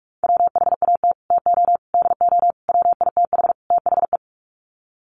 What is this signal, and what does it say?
Morse code, heard as a keyed tone, about 44 wpm, 690 Hz. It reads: WHAT TO DO WITH THE
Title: Sets of 5 Words Encore2 45wpm